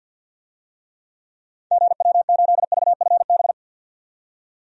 Morse code, heard as a keyed tone, about 52 wpm, 700 Hz. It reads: GW8VFB
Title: DXCC-100 Call Signs - 6 Characters - 50wpm